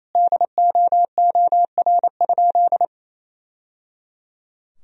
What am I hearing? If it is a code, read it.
Morse code, heard as a keyed tone, about 27 wpm, 700 Hz. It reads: DOOR?
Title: Sentences from Top 1000 Words - Review Part 4 - 28wpm